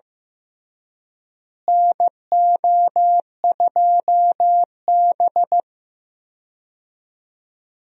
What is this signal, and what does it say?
Morse code, heard as a keyed tone, about 15 wpm, 700 Hz. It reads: NO2B
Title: USA Call Signs - 4 Characters - Repeated 3x - 15wpm